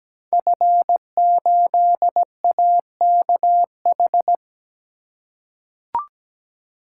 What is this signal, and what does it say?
Morse code, heard as a keyed tone, about 17 wpm, 700 Hz. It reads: F8AKH
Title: Code Groups - 5 Letter:Numbers - 17wpm